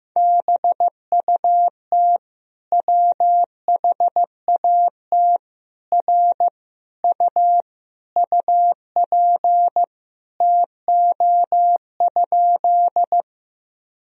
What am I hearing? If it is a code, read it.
Morse code, heard as a keyed tone, about 15 wpm, 700 Hz. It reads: BUT WHAT R U UP TO?
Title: Sentences from Top 100 Words and Abbreviations 15wpm